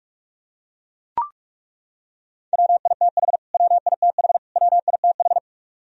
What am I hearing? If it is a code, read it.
Morse code, heard as a keyed tone, about 41 wpm, 700 Hz. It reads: WITH WITH WITH
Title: Top 100 Words Repeated 45wpm